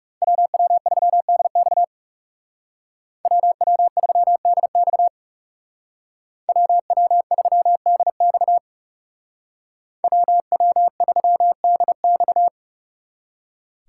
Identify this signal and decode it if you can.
Morse code, heard as a keyed tone, about 39 wpm, 700 Hz. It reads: WW3DX WW3DX WW3DX WW3DX